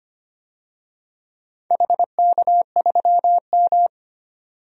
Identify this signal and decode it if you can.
Morse code, heard as a keyed tone, about 25 wpm, 700 Hz. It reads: HK3M
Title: DXCC-100 Call Signs - 4 Characters - 25wpm